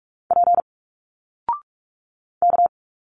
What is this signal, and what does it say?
Morse code, heard as a keyed tone, about 46 wpm, 700 Hz. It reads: P K
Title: Single Letters 45wpm